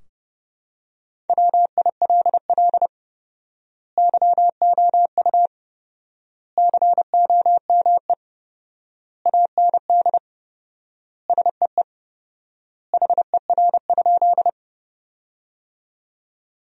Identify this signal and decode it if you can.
Morse code, heard as a keyed tone, about 30 wpm, 700 Hz. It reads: WILL YOU COME AND SEE HER?
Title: 4x Word-Spacing - Sentences from Top 100 Words - 30wpm